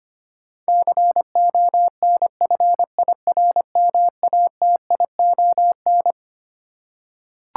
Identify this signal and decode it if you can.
Morse code, heard as a keyed tone, about 25 wpm, 700 Hz. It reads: CONFIRMATION
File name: Twelve Letter Words - 25wpm